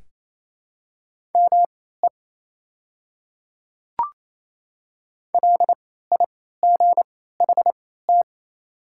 Morse code, heard as a keyed tone, about 28 wpm, spaced as for 16 wpm, 700 Hz. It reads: ME LIGHT
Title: 3x Character Spacing - Top 200 Words - 28wpm